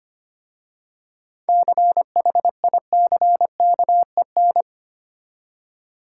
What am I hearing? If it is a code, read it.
Morse code, heard as a keyed tone, about 29 wpm, 700 Hz. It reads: CHICKEN
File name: Easy Words - Speed-Racing with Hints - 25wpm